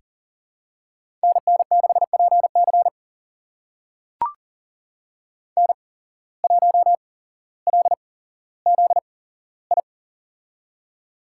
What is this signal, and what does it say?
Morse code, heard as a keyed tone, about 40 wpm, spaced as for 17 wpm, 700 Hz. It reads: NN6PC N1LZI
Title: USA Call Signs - 8x Character Spacing - 40wpm